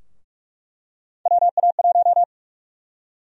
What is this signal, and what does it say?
Morse code, heard as a keyed tone, about 43 wpm, 700 Hz. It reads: WA1
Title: USA Call Sign Prefixes - 45wpm